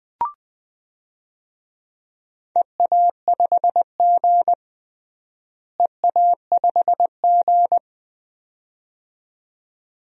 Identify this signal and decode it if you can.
Morse code, heard as a keyed tone, about 21 wpm, 700 Hz. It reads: EA5G EA5G
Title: DXCC-100 Call Signs - 4 Characters - Repeated 2x - 20wpm